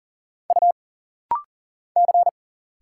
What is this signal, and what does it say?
Morse code, heard as a keyed tone, about 40 wpm, 700 Hz. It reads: U C
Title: Single Letters - Rapid-Fire 40wpm